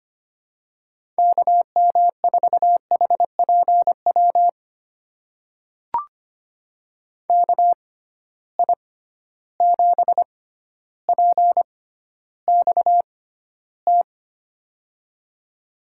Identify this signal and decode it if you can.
Morse code, heard as a keyed tone, about 25 wpm, spaced as for 13 wpm, 700 Hz. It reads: KM4HPW KI7PXT
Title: USA Call Signs - 6 Characters - 6x Character Spacing - 25wpm